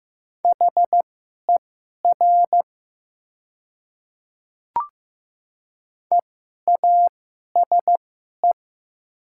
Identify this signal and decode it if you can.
Morse code, heard as a keyed tone, about 15 wpm, spaced as for 11 wpm, 700 Hz. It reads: HER EASE
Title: 2x Character Spacing - Top 300 Words - 15wpm